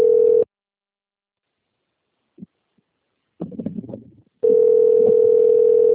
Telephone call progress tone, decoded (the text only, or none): ringback tone